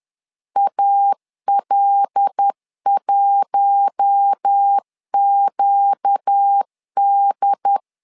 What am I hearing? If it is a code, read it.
Morse code, heard as a keyed tone, about 10 wpm, 800 Hz. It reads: AL1QD
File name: LEZIONI / 5 LBVC / 5 L 10wpm 2259